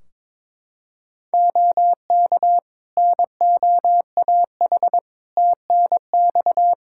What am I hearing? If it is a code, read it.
Morse code, heard as a keyed tone, about 22 wpm, 700 Hz. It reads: OK NOAH TNX
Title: Practice QSO - Exchange 4 - Ending - 22wpm